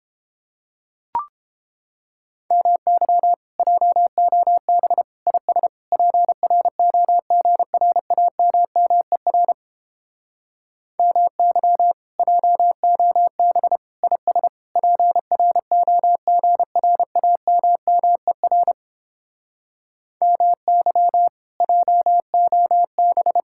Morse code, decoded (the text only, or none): MY JOB IS PROGRAMMER MY JOB IS PROGRAMMER MY JOB